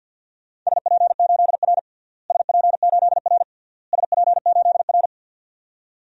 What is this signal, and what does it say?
Morse code, heard as a keyed tone, about 53 wpm, 700 Hz. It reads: SP8R SP8R SP8R
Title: DXCC-100 Call Signs - 4 Characters - Repeated 3x - 50wpm